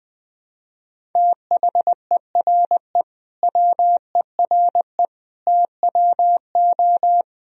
Morse code, heard as a keyed tone, about 20 wpm, 700 Hz. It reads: THERE WERE TWO
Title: Sentences from Top 1000 Words - Review Part 1 - 20wpm